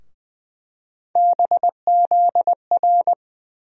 Morse code, heard as a keyed tone, about 20 wpm, 700 Hz. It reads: BZR
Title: USA Call Sign Postfixes - 3 Letters 20wpm